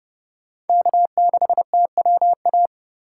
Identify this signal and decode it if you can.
Morse code, heard as a keyed tone, about 30 wpm, 700 Hz. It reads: K6TWA
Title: USA Call Signs - 5 Characters - 6x Character Spacing - 30wpm